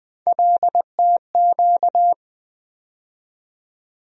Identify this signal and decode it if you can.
Morse code, heard as a keyed tone, about 20 wpm, 700 Hz. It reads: LTQ
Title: USA Call Sign Postfixes - 3 Letters 20wpm